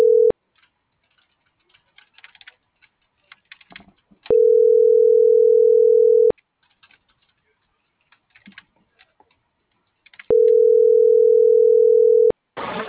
Ringback tone (North America), 440 + 480 Hz.